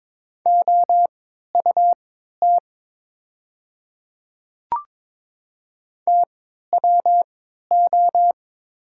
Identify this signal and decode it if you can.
Morse code, heard as a keyed tone, about 22 wpm, spaced as for 12 wpm, 700 Hz. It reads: OUT TWO
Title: 3x Character Spacing - Top 100 Words - 22wpm